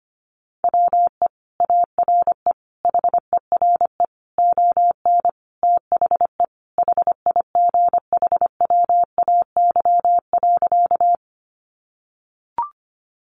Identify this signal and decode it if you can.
Morse code, heard as a keyed tone, about 25 wpm, 700 Hz. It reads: WE ARE HERE ON THE HIGHWAY.
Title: Sentences from Top 1400 Words 25wpm